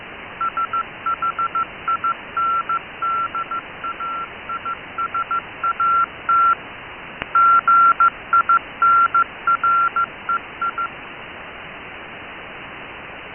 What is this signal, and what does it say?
Morse code, heard as a keyed tone, about 15 wpm, 1360 Hz. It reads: SHINDAISAT GINREI
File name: Shindaisat cw